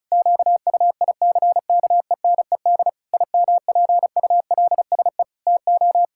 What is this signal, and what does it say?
Morse code, heard as a keyed tone, about 36 wpm, 700 Hz. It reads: QUICKENED IMPULSE TO
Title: Pangram Challenge v2 - 35wpm